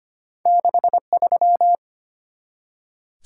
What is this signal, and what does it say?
Morse code, heard as a keyed tone, about 25 wpm, 690 Hz. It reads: 63
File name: Two Letter-Number Combinations - 25wpm